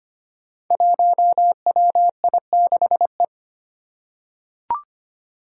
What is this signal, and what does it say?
Morse code, heard as a keyed tone, about 25 wpm, 700 Hz. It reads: 1WI6E
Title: Five Challenging Letter-Number Combinations 25wpm